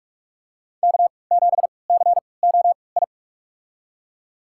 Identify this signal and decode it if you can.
Morse code, heard as a keyed tone, about 42 wpm, spaced as for 25 wpm, 700 Hz. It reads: K7COI
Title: USA Call Signs - 5 Characters - 3x Character Spacing - 45wpm